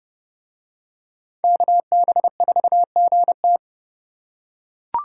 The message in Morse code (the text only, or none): KB4GT